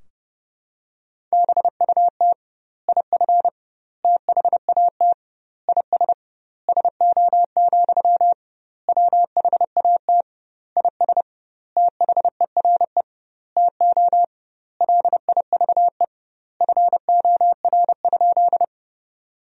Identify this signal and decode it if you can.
Morse code, heard as a keyed tone, about 30 wpm, 700 Hz. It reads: BUT IF THAT IS SO, WHAT IS THERE TO LIVE FOR?
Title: 2x Word-Spacing - Sentences from Top 200 Words - 30wpm